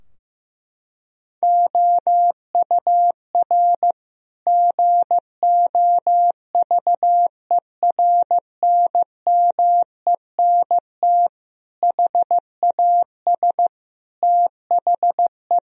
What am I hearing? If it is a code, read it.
Morse code, heard as a keyed tone, about 15 wpm, 700 Hz. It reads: OUR GOVERNMENT HAS THE
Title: Sentences from Top 1000 Words - Review Part 3 - 15wpm